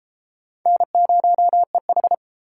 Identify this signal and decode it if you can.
Morse code, heard as a keyed tone, about 33 wpm, 700 Hz. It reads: N0EH